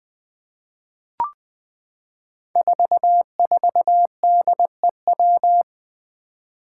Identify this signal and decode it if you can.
Morse code, heard as a keyed tone, about 20 wpm, 700 Hz. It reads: 44DEW
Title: Five Challenging Letter-Number Combinations 20wpm